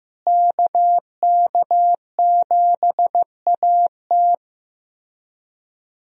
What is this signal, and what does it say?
Morse code, heard as a keyed tone, about 15 wpm, 700 Hz. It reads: KK7AT
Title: USA Call Signs - 5 Characters - Repeated 3x - 15wpm